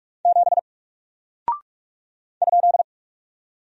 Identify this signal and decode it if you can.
Morse code, heard as a keyed tone, about 44 wpm, 700 Hz. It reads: 7 ?
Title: Single Character (Letter:Number:Punctuation:Prosign) 45wpm